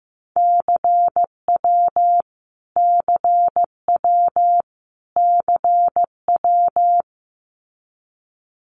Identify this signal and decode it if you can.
Morse code, heard as a keyed tone, about 15 wpm, 700 Hz. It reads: CW CW CW
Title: Abbreviations Repeated 3x - 15wpm